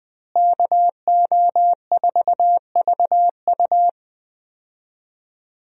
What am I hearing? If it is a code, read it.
Morse code, heard as a keyed tone, about 20 wpm, 700 Hz. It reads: KO4VU
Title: USA Call Signs - 5 Characters - Repeated 2x - 20wpm